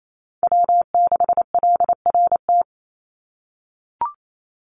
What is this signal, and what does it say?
Morse code, heard as a keyed tone, about 28 wpm, 700 Hz. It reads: W6LRT
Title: USA Call Signs 28wpm